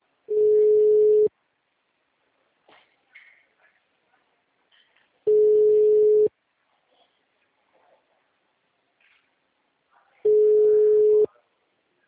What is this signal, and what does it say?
Ringback tone (Europe, 425 Hz), with speech.
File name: out- -2020- - - .74511